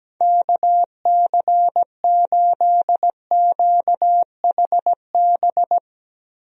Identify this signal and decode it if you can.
Morse code, heard as a keyed tone, about 17 wpm, 700 Hz. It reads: KC8QHB
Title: USA Call Signs - 6 Characters - Repeated 3x - 17wpm